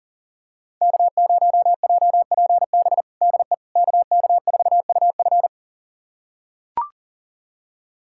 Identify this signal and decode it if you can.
Morse code, heard as a keyed tone, about 40 wpm, 690 Hz. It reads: K0JPB DE KK4UF
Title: USA Callsign DE Callsign - 5 Characters - 40wpm